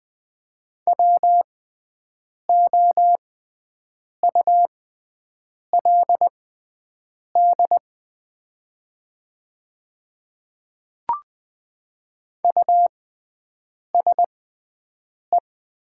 Morse code, heard as a keyed tone, about 20 wpm, spaced as for 7 wpm, 700 Hz. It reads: WOULD USE
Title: 6x Character Spacing - Top 300 Words - 20wpm